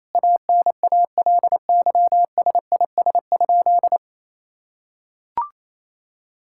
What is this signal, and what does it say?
Morse code, heard as a keyed tone, about 28 wpm, 700 Hz. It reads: ANALYSIS?
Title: Sentences from Top 1000 Words - Review Part 5 - 28wpm